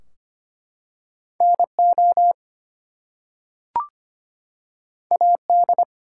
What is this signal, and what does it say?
Morse code, heard as a keyed tone, about 25 wpm, 700 Hz. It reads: NO AD
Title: Two Letter Words - 25wpm